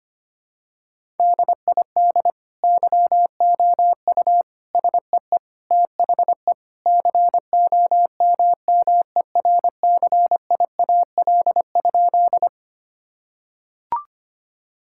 Morse code, heard as a keyed tone, about 25 wpm, 700 Hz. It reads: DID YOU SEE THE COMMERCIAL?
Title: Sentences from Top 1000 Words - Review Part 5 - 25wpm